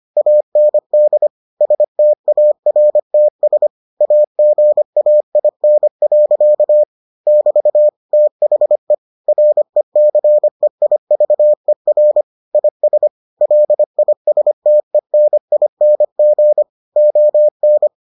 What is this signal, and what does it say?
Morse code, heard as a keyed tone, about 25 wpm, 600 Hz. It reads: AND STARTS AGAIN. <BT> THE RECEIVER IS LISTENING ON